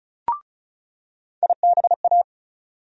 Morse code, heard as a keyed tone, about 36 wpm, 700 Hz. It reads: IBA
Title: USA Call Sign Postfixes - 3 Letters 35wpm